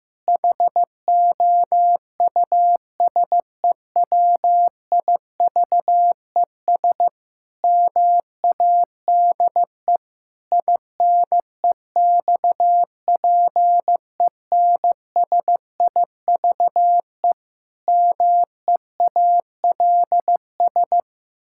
Morse code, heard as a keyed tone, about 15 wpm, 700 Hz. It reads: HOUSEWIVES MADE INEXPENSIVE MEALS
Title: Pangram Challenge v2 - 15wpm